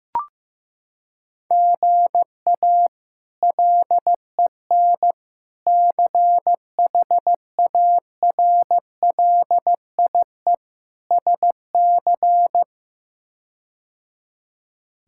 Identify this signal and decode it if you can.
Morse code, heard as a keyed tone, about 15 wpm, 700 Hz. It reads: GA LEN CHARLIE SC
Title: K1USN SST QSO Practice - 15wpm